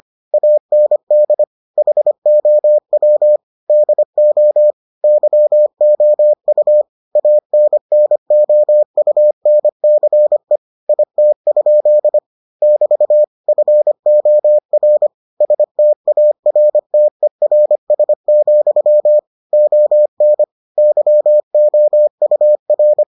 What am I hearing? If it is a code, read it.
Morse code, heard as a keyed tone, about 25 wpm, 600 Hz. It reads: AND HOW DO YOU ANNOUNCE IT? <BT> FOR STARTERS, ON YOUR